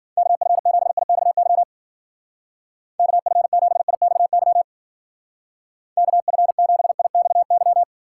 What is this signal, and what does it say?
Morse code, heard as a keyed tone, about 52 wpm, 700 Hz. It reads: KF7IXY KF7IXY KF7IXY